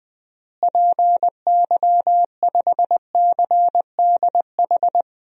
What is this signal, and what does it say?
Morse code, heard as a keyed tone, about 20 wpm, 700 Hz. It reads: PY5CDH
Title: DXCC-100 Call Signs - 6 Characters - Repeated 3x - 20wpm